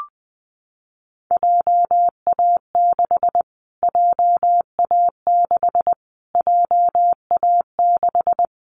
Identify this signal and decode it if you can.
Morse code, heard as a keyed tone, about 20 wpm, 700 Hz. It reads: JA6 JA6 JA6
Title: Callsign Prefixes for Top 25 DXCC - Repeated 3x - 20wpm